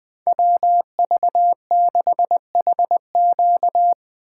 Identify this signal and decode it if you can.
Morse code, heard as a keyed tone, about 20 wpm, 700 Hz. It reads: WV6HQ
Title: USA Call Signs - 5 Characters - 8x Character Spacing - 20wpm